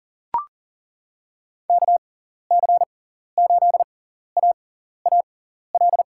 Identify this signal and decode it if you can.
Morse code, heard as a keyed tone, about 40 wpm, spaced as for 14 wpm, 700 Hz. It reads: KC8AAL
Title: USA Call Signs - 6 Characters - 6x Character Spacing - 40wpm